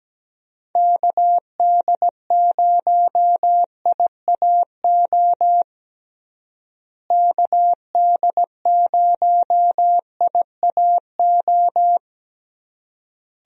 Morse code, heard as a keyed tone, about 17 wpm, 700 Hz. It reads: KD0IAO KD0IAO
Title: USA Call Signs - 6 Characters - Repeated 3x - 17wpm